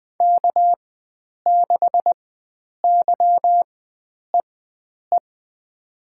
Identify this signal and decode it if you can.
Morse code, heard as a keyed tone, about 20 wpm, spaced as for 9 wpm, 700 Hz. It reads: K6YEE
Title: USA Call Signs - 5 Characters - 4x Character Spacing - 20wpm